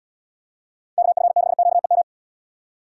Morse code, heard as a keyed tone, about 70 wpm, 700 Hz. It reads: KF6QEM